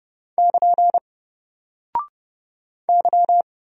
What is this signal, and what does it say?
Morse code, heard as a keyed tone, about 31 wpm, 700 Hz. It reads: <KN> Y
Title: Single Character (Letter:Number:Punctuation:Prosign) 30wpm